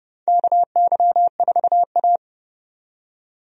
Morse code, heard as a keyed tone, about 29 wpm, 700 Hz. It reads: KY4A
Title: USA Call Signs - 4 Characters - 30wpm